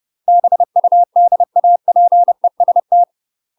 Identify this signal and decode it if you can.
Morse code, heard as a keyed tone, about 30 wpm, 700 Hz. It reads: BUDAPEST
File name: UK MASTER QSOs TRAINING/30WPM